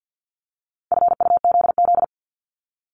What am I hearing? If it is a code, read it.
Morse code, heard as a keyed tone, about 50 wpm, 700 Hz. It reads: FUZZ
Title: Multi-letter Phonemes Repeated 3x 50wpm